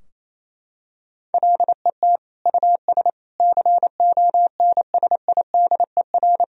Morse code, heard as a keyed tone, about 28 wpm, 700 Hz. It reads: LET US CONSIDER
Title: Sentences from Top 1000 Words - Review Part 4 - 28wpm